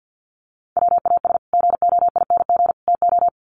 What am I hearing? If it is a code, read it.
Morse code, heard as a keyed tone, about 50 wpm, 700 Hz. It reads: WAS GOING TO
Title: Sets of 3 Words 50wpm